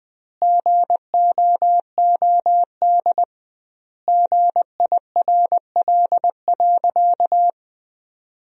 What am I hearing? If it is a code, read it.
Morse code, heard as a keyed tone, about 20 wpm, 700 Hz. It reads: GOOD GIRL.
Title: 2x Word-Spacing - Sentences from Top 400 Words - 20wpm